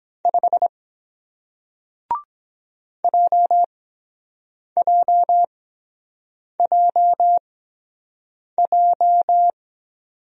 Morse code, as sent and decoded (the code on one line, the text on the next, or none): ..... / .--- .--- .--- .---
5 JJJJ